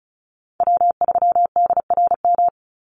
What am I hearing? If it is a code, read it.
Morse code, heard as a keyed tone, about 34 wpm, 700 Hz. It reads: W3DRM
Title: USA Call Signs 35wpm